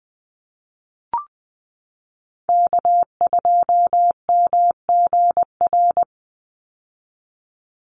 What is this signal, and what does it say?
Morse code, heard as a keyed tone, about 19 wpm, 700 Hz. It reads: K2MGR
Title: Call Signs - 20wpm